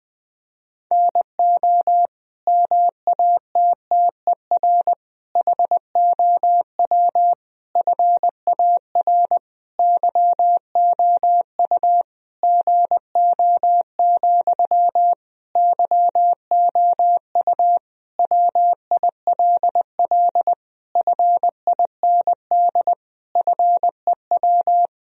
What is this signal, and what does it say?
Morse code, heard as a keyed tone, about 21 wpm, 700 Hz. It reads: NO MATTER HOW FAR YOU GO, YOU WILL FIND FEW
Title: Sentences from Top 1000 Words - Review Part 1 - 20wpm